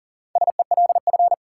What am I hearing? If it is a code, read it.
Morse code, heard as a keyed tone, about 40 wpm, 700 Hz. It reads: SELF
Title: Sentences from Top 1000 Words - Review Part 1 - New Word Drill - 40wpm